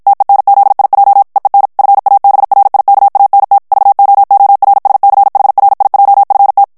Morse code, heard as a keyed tone, about 56 wpm, 800 Hz. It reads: TENZIJ EEN FABRIKANT VOORSCHRIJFT
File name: HOLLANDAIS texte clair 2 vitesse 50 mots minute